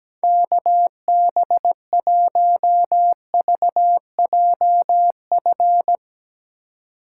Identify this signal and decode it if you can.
Morse code, heard as a keyed tone, about 17 wpm, 700 Hz. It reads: KB1VJF
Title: USA Call Signs - 6 Characters - 3x Character Spacing - 17wpm